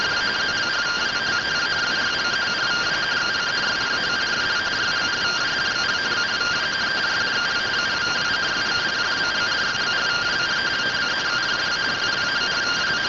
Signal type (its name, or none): two-tone teleprinter signal